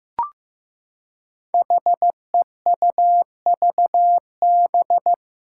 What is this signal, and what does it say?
Morse code, heard as a keyed tone, about 15 wpm, 700 Hz. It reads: HEUVB
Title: Five Challenging Letter-Number Combinations 15wpm